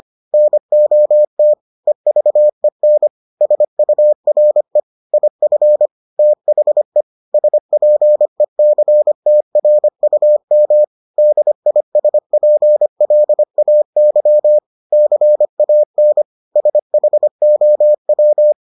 Morse code, heard as a keyed tone, about 25 wpm, 600 Hz. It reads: NOT EVEN SURE IF THE SPECTRUM DISPLAY CAN SHOW